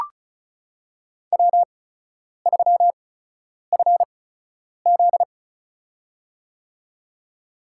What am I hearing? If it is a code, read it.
Morse code, heard as a keyed tone, about 34 wpm, spaced as for 10 wpm, 700 Hz. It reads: W3FZ
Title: USA Call Signs - 4 Characters - 8x Character Spacing - 35wpm